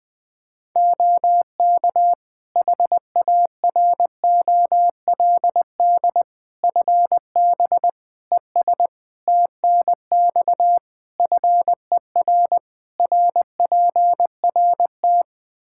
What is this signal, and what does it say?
Morse code, heard as a keyed tone, about 20 wpm, 700 Hz. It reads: OK HAROLD FB ES TNX FER RPRT
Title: Practice QSO - Exchange 2 - Info - 20wpm